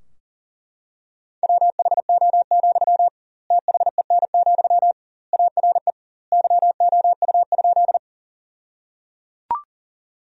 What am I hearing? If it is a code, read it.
Morse code, heard as a keyed tone, about 37 wpm, 700 Hz. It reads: WHO, THEN, ARE YOU?
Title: 2x Word-Spacing - Sentences from Top 100 Words - 40wpm